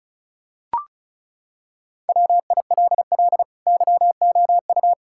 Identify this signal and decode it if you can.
Morse code, heard as a keyed tone, about 36 wpm, 700 Hz. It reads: WILL YOU
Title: Sentences from Top 1000 Words - Review Part 5 - 35wpm